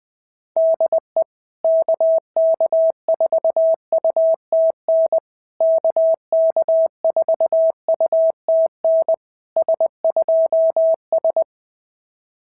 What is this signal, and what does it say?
Morse code, heard as a keyed tone, about 18 wpm, 650 Hz. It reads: DE KK4UTN KK4UTN S2S
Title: SOTA QSO Practice 20wpm